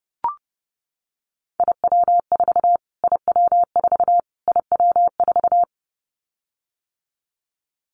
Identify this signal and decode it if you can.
Morse code, heard as a keyed tone, about 30 wpm, 700 Hz. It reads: IW4 IW4 IW4
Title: Callsign Prefixes for Top 25 DXCC - Repeated 3x - 30wpm